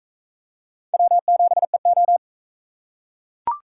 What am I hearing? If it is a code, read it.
Morse code, heard as a keyed tone, about 41 wpm, 700 Hz. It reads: W7EO